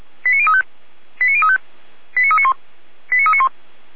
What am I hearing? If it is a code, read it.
Five-tone selective call (ZVEI), heard as quick runs of five tones, tones of about 70 ms. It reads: 89035 89035 89391 89391
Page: /455/SDR/ZVEI/